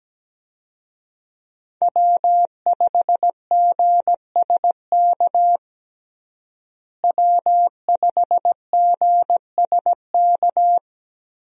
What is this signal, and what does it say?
Morse code, heard as a keyed tone, about 17 wpm, 700 Hz. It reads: W5GSK W5GSK
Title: USA Call Signs - 5 Characters - Repeated 2x - 17wpm